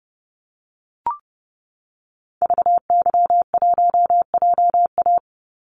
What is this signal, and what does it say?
Morse code, heard as a keyed tone, about 30 wpm, 700 Hz. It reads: VY1JA
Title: Call Signs Repeated 3x - 30wpm